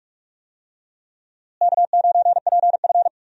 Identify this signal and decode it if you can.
Morse code, heard as a keyed tone, about 46 wpm, 700 Hz. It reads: K9PF
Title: USA Call Signs - 4 Characters - Repeated 3x - 45wpm